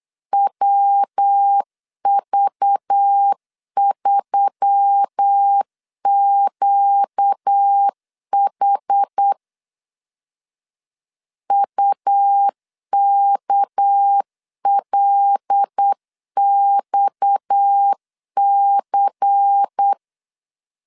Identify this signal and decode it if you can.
Morse code, heard as a keyed tone, about 8 wpm, 800 Hz. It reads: WV3QH UKLXC
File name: CW / DET / 1 6 1500C S B 8wpm 5142